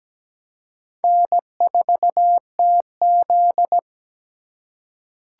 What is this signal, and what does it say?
Morse code, heard as a keyed tone, about 17 wpm, 700 Hz. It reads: N4TZ
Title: K1USN SST QSO Practice - 17wpm